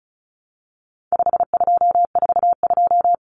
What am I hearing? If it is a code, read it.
Morse code, heard as a keyed tone, about 34 wpm, 700 Hz. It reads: 5242
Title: Numbers - 4-Digit 35wpm